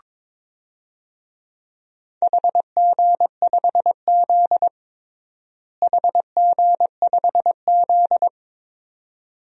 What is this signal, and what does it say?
Morse code, heard as a keyed tone, about 22 wpm, 700 Hz. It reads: HG5Z HG5Z
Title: DXCC-100 Call Signs - 4 Characters - Repeated 2x - 22wpm